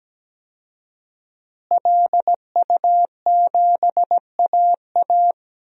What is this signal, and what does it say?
Morse code, heard as a keyed tone, about 17 wpm, 700 Hz. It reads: LU7AA
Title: DXCC-100 Call Signs - 5 Characters - Repeated 3x - 17wpm